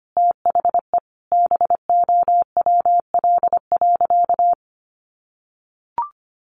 Morse code, heard as a keyed tone, about 25 wpm, 700 Hz. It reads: THE BOWL.
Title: Sentences from Top 1200 Words 25wpm